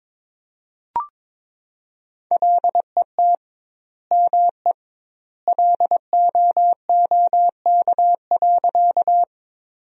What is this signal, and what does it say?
Morse code, heard as a keyed tone, about 22 wpm, 700 Hz. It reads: LET ME LOOK.
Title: 2x Word-Spacing - Sentences from Top 300 Words - 22wpm